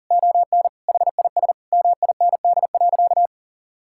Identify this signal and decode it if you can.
Morse code, heard as a keyed tone, about 40 wpm, 700 Hz. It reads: ON HIS MIND.
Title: Sentences from Top 1000 Words - Review Part 5 - 40wpm